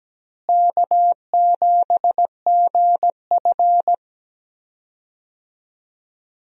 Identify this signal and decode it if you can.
Morse code, heard as a keyed tone, about 17 wpm, 700 Hz. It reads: K7GF
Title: USA Call Signs - 4 Characters - Repeated 3x - 17wpm